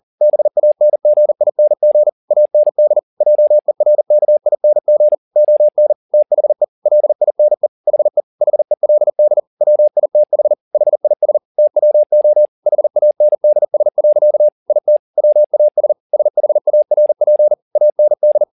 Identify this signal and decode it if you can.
Morse code, heard as a keyed tone, about 40 wpm, 600 Hz. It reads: BANGING AND JERKING ON THE LINE HE HELD WITH HIS TWO HANDS. IT WAS SHARP AND
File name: the old man and the sea 2 40wpm